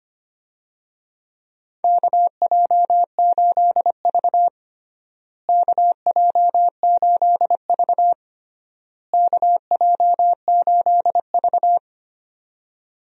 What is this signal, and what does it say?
Morse code, heard as a keyed tone, about 25 wpm, 700 Hz. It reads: KJ8V KJ8V KJ8V
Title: USA Call Signs - 4 Characters - Repeated 3x - 25wpm